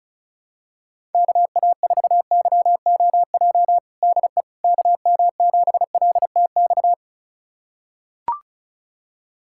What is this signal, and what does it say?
Morse code, heard as a keyed tone, about 36 wpm, 700 Hz. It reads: KA4YOJ DE KM7LTX
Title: USA Callsign DE Callsign 35wpm